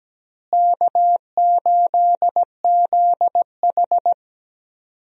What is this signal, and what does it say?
Morse code, heard as a keyed tone, about 17 wpm, 700 Hz. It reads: K8ZH
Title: USA Call Signs - 4 Characters - Repeated 3x - 17wpm